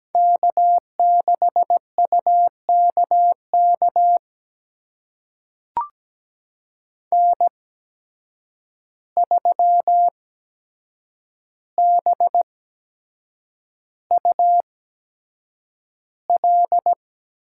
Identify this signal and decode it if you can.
Morse code, heard as a keyed tone, about 17 wpm, spaced as for 7 wpm, 700 Hz. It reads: K6UKK N3BUL
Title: USA Call Signs - 5 Characters - 8x Character Spacing - 17wpm